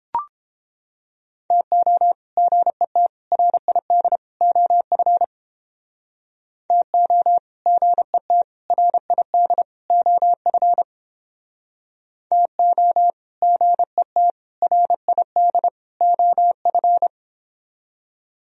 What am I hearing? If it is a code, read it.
Morse code, heard as a keyed tone, about 31 wpm, 700 Hz. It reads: TO GET RID OF TO GET RID OF TO GET RID OF